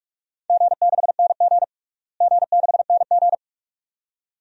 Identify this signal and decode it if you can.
Morse code, heard as a keyed tone, about 46 wpm, 700 Hz. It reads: G6NG G6NG
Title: DXCC-100 Call Signs - 4 Characters - Repeated 2x - 45wpm